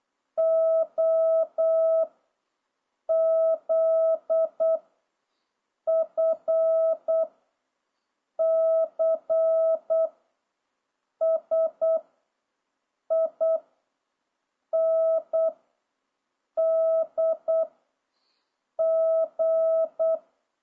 Morse code, heard as a keyed tone, about 8 wpm, 660 Hz. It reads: O Z F C S I N D G